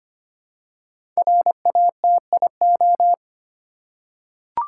Morse code, heard as a keyed tone, about 26 wpm, 700 Hz. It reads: RATIO
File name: 5 Letter Words - First 3 Letters Given - 25wpm